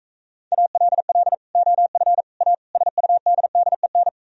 Morse code, heard as a keyed tone, about 45 wpm, 700 Hz. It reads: ALL OF A SUDDEN